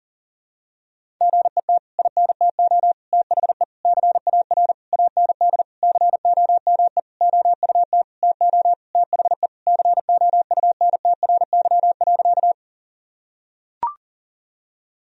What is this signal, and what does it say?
Morse code, heard as a keyed tone, about 40 wpm, 700 Hz. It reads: GET INTO THE CAR AND COME OUT TO THE COUNTRY.
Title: Sentences from Top 1000 Words - Review Part 1 - 40wpm